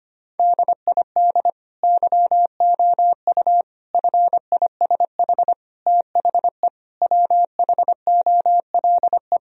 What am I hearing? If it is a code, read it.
Morse code, heard as a keyed tone, about 25 wpm, 700 Hz. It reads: DID YOU FISH THE WHOLE
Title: Sentences from Top 1000 Words - Review Part 1 - 25wpm